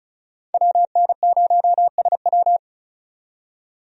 Morse code, heard as a keyed tone, about 36 wpm, 700 Hz. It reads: WN0SW
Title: USA Call Signs - 5 Characters - 35wpm